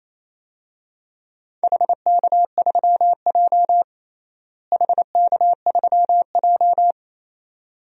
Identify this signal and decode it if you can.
Morse code, heard as a keyed tone, about 28 wpm, 700 Hz. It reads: HK3J HK3J
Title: DXCC-100 Call Signs - 4 Characters - Repeated 2x - 28wpm